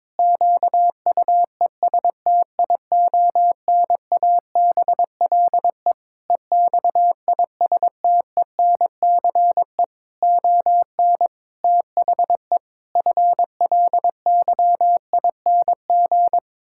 Morse code, heard as a keyed tone, about 22 wpm, 700 Hz. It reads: QUESTIONABLE EXISTENCE ON THE FLYING
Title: Pangram Challenge v2 - 22wpm